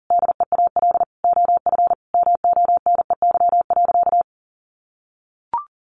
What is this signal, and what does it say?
Morse code, heard as a keyed tone, about 37 wpm, 700 Hz. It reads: DEAL OF MONEY.
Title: Sentences from Top 1300 Words 40wpm